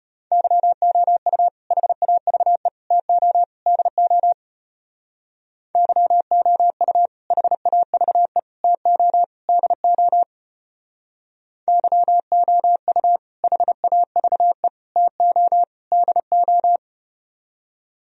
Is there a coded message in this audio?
Yes: Morse code from 0.3 to 16.8 s, heard as a keyed tone, about 33 wpm, 700 Hz. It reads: YOU HAVE TO DO YOU HAVE TO DO YOU HAVE TO DO